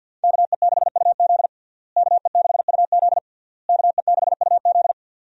Morse code, heard as a keyed tone, about 50 wpm, 700 Hz. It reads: KE6UZ KE6UZ KE6UZ
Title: USA Call Signs - 5 Characters - Repeated 3x - 50wpm